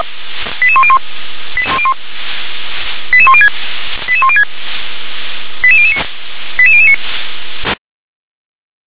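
Five-tone selective call (ZVEI), heard as quick runs of five tones, tones of about 70 ms. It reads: 88101 88106 88106 88009 88009
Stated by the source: ZVEI five-tone selective call received with an SDR.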